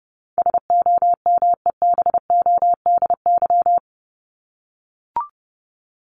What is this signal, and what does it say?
Morse code, heard as a keyed tone, about 30 wpm, 700 Hz. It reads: SOMEBODY
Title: Top 1751-2000 Words 30wpm